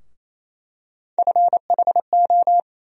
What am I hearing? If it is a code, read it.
Morse code, heard as a keyed tone, about 28 wpm, 700 Hz. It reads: FHO
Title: USA Call Sign Postfixes - 3 Letters 28wpm